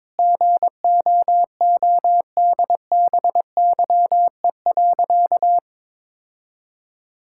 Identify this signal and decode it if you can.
Morse code, heard as a keyed tone, about 22 wpm, 700 Hz. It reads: GOODBYE.
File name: practice / 22wpm / Sentences with 4-Word Phrases - 22wpm